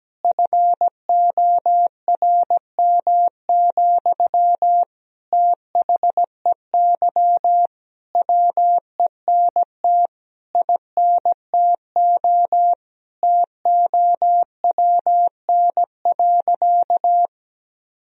Morse code, heard as a keyed tone, about 17 wpm, 700 Hz. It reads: FORM, THEY WENT INTO TOWN.
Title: Sentences from Top 1000 Words - Review Part 1 - 17wpm